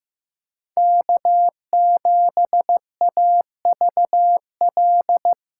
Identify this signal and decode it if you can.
Morse code, heard as a keyed tone, about 15 wpm, 700 Hz. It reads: K7AVL
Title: USA Callsign DE Callsign - 5 Characters - 15wpm